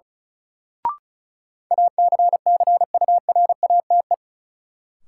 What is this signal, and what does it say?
Morse code, heard as a keyed tone, about 34 wpm, 700 Hz. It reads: ACCURATE
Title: Sentences from Top 1200 Words - New Word Drill - 35wpm